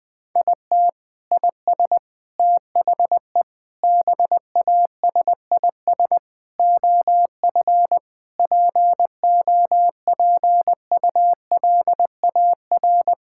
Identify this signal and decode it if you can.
Morse code, heard as a keyed tone, about 20 wpm, 700 Hz. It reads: IT IS THE BASIS OF POPULAR
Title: Sentences from Top 1000 Words - Review Part 5 - 20wpm